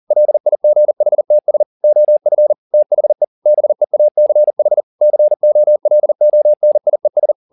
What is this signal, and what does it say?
Morse code, heard as a keyed tone, about 40 wpm, 600 Hz. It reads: LIGHTS OF THE BEACH COLONIES
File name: the old man and the sea 2 40wpm